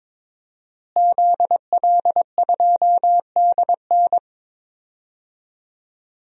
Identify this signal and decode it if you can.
Morse code, heard as a keyed tone, about 21 wpm, 700 Hz. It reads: ZL2DN
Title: DXCC-100 Call Signs - 5 Characters - Repeated 2x - 22wpm